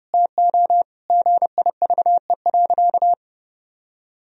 Morse code, heard as a keyed tone, about 30 wpm, 700 Hz. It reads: TO GIVE.
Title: Sentences from Top 1000 Words - Review Part 4 - 30wpm